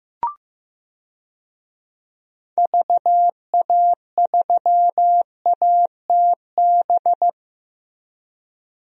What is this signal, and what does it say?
Morse code, heard as a keyed tone, about 15 wpm, 700 Hz. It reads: VA3ATB
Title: DXCC-100 Call Signs - 6 Characters - Repeated 3x - 15wpm